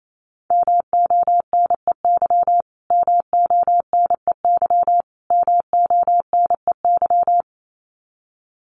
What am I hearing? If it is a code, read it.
Morse code, heard as a keyed tone, about 28 wpm, 700 Hz. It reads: MONEY MONEY MONEY
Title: Top 500 Words Repeated 28wpm